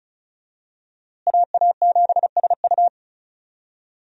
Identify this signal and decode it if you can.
Morse code, heard as a keyed tone, about 36 wpm, 700 Hz. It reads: AA7SU
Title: USA Call Signs - 5 Characters - Repeated 3x - 35wpm